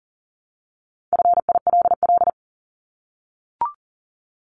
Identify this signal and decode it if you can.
Morse code, heard as a keyed tone, about 40 wpm, 700 Hz. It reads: FILL
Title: Top 500 Words Repeated 40wpm